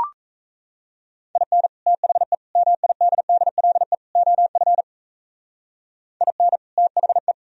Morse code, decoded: IN THE MIDDLE OF IN THE

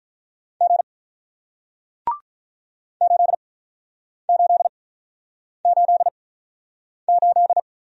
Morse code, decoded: G 8 8 8 8

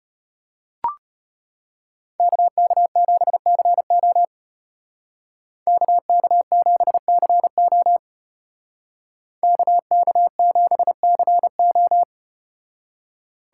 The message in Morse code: KK7CO KK7CO KK7CO

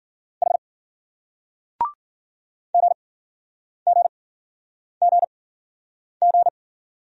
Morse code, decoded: H GGGG

